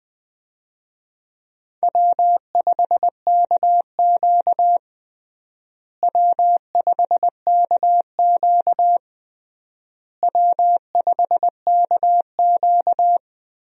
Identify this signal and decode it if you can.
Morse code, heard as a keyed tone, about 20 wpm, 700 Hz. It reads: W5KQ W5KQ W5KQ
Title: USA Call Signs - 4 Characters - Repeated 3x - 20wpm